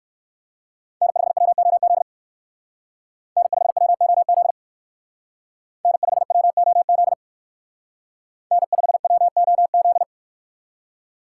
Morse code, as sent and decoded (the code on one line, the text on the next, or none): -. ..... .-- --- --.. / -. ..... .-- --- --.. / -. ..... .-- --- --.. / -. ..... .-- --- --..
N5WOZ N5WOZ N5WOZ N5WOZ